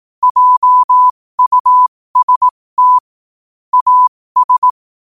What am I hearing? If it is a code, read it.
Morse code, heard as a keyed tone, about 18 wpm, spaced as for 15 wpm, 1000 Hz. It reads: JUST AS
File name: qotdfiles / 15wpm